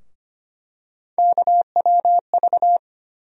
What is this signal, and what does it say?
Morse code, heard as a keyed tone, about 25 wpm, 700 Hz. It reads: KWV
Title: USA Call Sign Postfixes - 3 Letters 25wpm